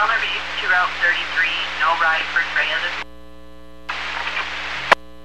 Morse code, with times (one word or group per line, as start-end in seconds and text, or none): none